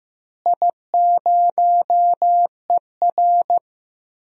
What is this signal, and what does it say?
Morse code, heard as a keyed tone, about 15 wpm, 700 Hz. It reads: I0ER
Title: DXCC-100 Call Signs - 4 Characters - Repeated 2x - 15wpm